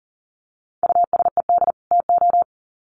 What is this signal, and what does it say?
Morse code, heard as a keyed tone, about 40 wpm, 700 Hz. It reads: USED TO
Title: Sets of 2 Words 40wpm